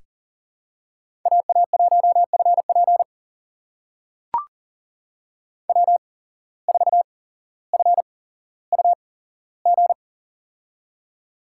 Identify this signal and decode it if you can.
Morse code, heard as a keyed tone, about 38 wpm, spaced as for 17 wpm, 700 Hz. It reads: AA1FP W4FUG
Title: USA Call Signs - 5 Characters - 8x Character Spacing - 40wpm